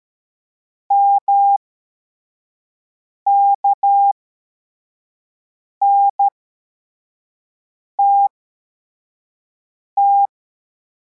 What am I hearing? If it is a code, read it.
Morse code, heard as a keyed tone, about 13 wpm, spaced as for 4 wpm, 800 Hz. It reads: MKNTT
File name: formation / cw / Text21